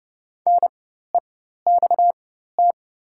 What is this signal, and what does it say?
Morse code, heard as a keyed tone, about 30 wpm, spaced as for 14 wpm, 700 Hz. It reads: NEXT
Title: 4x Character Spacing - Top 500 Words - 30wpm